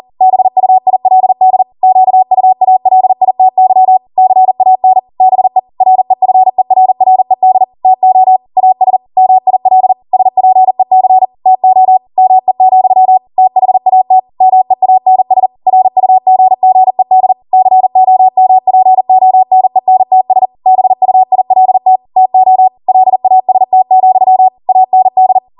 Morse code, heard as a keyed tone, about 40 wpm, 750 Hz. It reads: BUILD QUALITY CAN BE REFERRED TO AS MIL SPEC TO ME, THAT MEANS RUGGED COMPONENTS BUILT TO LAST, AND